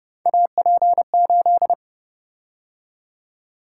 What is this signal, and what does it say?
Morse code, heard as a keyed tone, about 30 wpm, 700 Hz. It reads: AP8